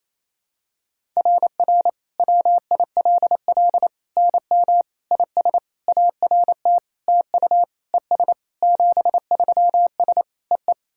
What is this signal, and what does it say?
Morse code, heard as a keyed tone, about 28 wpm, 700 Hz. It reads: RR WILL NM IS ART TU ES 73S EE
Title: SOTA QSO Practice 28wpm